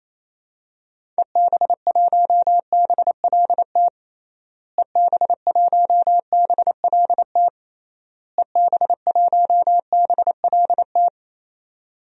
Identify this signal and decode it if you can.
Morse code, heard as a keyed tone, about 28 wpm, 700 Hz. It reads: EB1BLT EB1BLT EB1BLT
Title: DXCC-100 Call Signs - 6 Characters - Repeated 3x - 28wpm